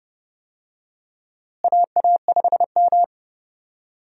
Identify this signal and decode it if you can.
Morse code, heard as a keyed tone, about 30 wpm, 700 Hz. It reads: AA5M
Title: USA Call Signs - 4 Characters - Repeated 2x - 30wpm